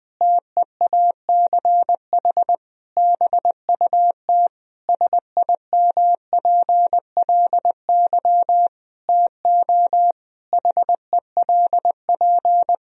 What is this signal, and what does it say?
Morse code, heard as a keyed tone, about 20 wpm, 700 Hz. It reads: TEACH BUT SIMPLY TO HELP
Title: Sentences from Top 1000 Words - Review Part 4 - 20wpm